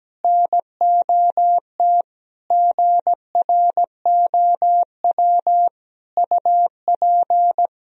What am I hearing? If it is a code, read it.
Morse code, heard as a keyed tone, about 17 wpm, 700 Hz. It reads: NOT GROW UP
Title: Sentences from Top 1000 Words - Review Part 1 - 17wpm